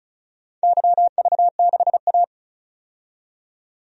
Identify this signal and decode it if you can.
Morse code, heard as a keyed tone, about 34 wpm, 700 Hz. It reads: YV6A
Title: DXCC-100 Call Signs - 4 Characters - 35wpm